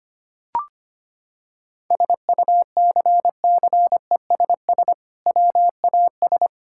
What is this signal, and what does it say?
Morse code, heard as a keyed tone, about 25 wpm, 700 Hz. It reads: SUCCESS WAS
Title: Sentences from Top 1000 Words - Review Part 6 - 25wpm